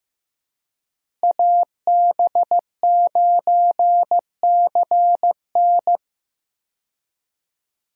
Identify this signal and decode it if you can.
Morse code, heard as a keyed tone, about 15 wpm, 700 Hz. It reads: AB9CN
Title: USA Call Signs - 5 Characters - Repeated 3x - 15wpm